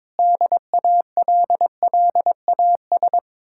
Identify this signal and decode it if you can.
Morse code, heard as a keyed tone, about 22 wpm, 700 Hz. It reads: DALLAS
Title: Speed-Racing Common QSO Phrases - 22wpm